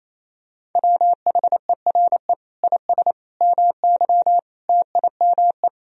Morse code, heard as a keyed tone, about 27 wpm, 700 Hz. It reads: WHERE IS MY TIME
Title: Sentences from Top 1000 Words - Review Part 2 - 28wpm